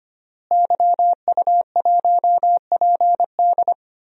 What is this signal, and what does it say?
Morse code, heard as a keyed tone, about 25 wpm, 700 Hz. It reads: YU1PD
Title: DXCC-100 Call Signs - 5 Characters - Repeated 2x - 25wpm